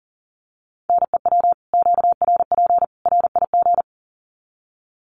Morse code, heard as a keyed tone, about 40 wpm, 700 Hz. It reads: NEW QRP RIG
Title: Common QSO Phrases 40wpm